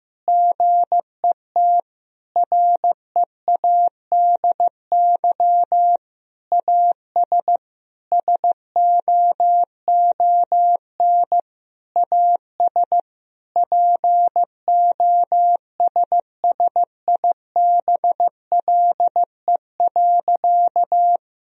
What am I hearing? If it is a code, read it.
Morse code, heard as a keyed tone, about 16 wpm, 700 Hz. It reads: GET READY AS SOON AS POSSIBLE.
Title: Sentences from Top 1000 Words - Review Part 2 - 15wpm